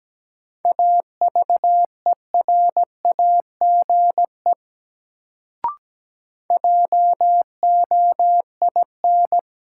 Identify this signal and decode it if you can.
Morse code, heard as a keyed tone, about 17 wpm, 700 Hz. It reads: AVERAGE JOIN
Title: Sentences from Top 1000 Words - Review Part 6 - New Word Drill - 17wpm